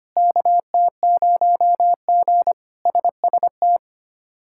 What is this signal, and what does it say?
Morse code, heard as a keyed tone, about 25 wpm, 700 Hz. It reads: KT0G SST
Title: K1USN SST QSO Practice - 25wpm